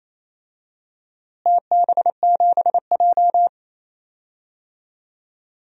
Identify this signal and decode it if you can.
Morse code, heard as a keyed tone, about 28 wpm, 700 Hz. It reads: TB7J
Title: DXCC-100 Call Signs - 4 Characters - Repeated 3x - 28wpm